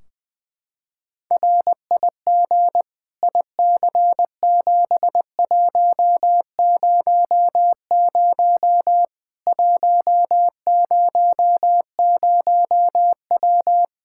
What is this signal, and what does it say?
Morse code, heard as a keyed tone, about 20 wpm, 700 Hz. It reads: RIG IC7100 100W
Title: Practice QSO - Exchange 2 - Info - 20wpm